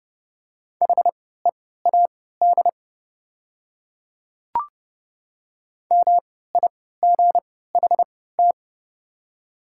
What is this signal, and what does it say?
Morse code, heard as a keyed tone, about 30 wpm, spaced as for 17 wpm, 700 Hz. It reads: HEAD MIGHT
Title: 3x Character Spacing - Top 300 Words - 30wpm